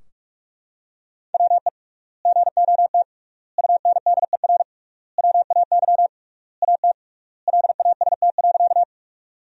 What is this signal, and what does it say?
Morse code, heard as a keyed tone, about 43 wpm, 700 Hz. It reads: WE GOT UNDER WAY AT LAST.
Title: 3x Word-Spacing - Sentences from Top 300 Words - 45wpm